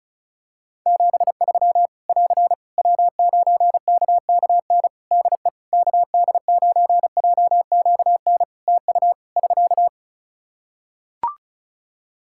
Morse code, heard as a keyed tone, about 34 wpm, 700 Hz. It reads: 73 <AR> W9KKN DE KD9JQN TU <SK>
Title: Practice QSO - Exchange 4 - Ending - 35wpm